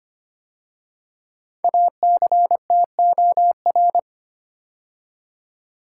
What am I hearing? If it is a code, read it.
Morse code, heard as a keyed tone, about 25 wpm, 700 Hz. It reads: ACTOR
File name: 5 Letter Words - First 3 Letters Given - 25wpm